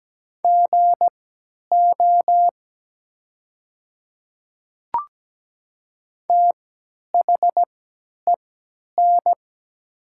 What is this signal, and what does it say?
Morse code, heard as a keyed tone, about 17 wpm, spaced as for 10 wpm, 700 Hz. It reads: GO THEN
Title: 3x Character Spacing - Top 100 Words - 17wpm